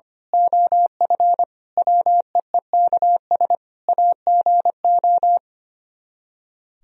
Morse code, heard as a keyed tone, about 24 wpm, 700 Hz. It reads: OF WEEKS AGO
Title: Speed-Racing - Sets of 5 Words 25wpm